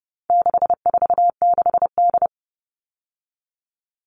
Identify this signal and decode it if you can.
Morse code, heard as a keyed tone, about 30 wpm, 700 Hz. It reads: 646D
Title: All Cut Numbers - 4-Digit 30wpm